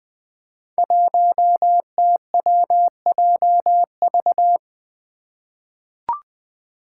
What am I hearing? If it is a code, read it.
Morse code, heard as a keyed tone, about 20 wpm, 700 Hz. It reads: 1TWJV
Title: Five Challenging Letter-Number Combinations 20wpm